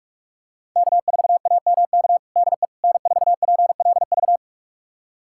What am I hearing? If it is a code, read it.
Morse code, heard as a keyed tone, about 42 wpm, 700 Hz. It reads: K4AMK DE N4PLV
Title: USA Callsign DE Callsign - 5 Characters - 45wpm